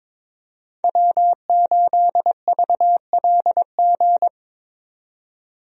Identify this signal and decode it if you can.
Morse code, heard as a keyed tone, about 21 wpm, 700 Hz. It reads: W8VLG
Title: USA Call Signs - 5 Characters - 22wpm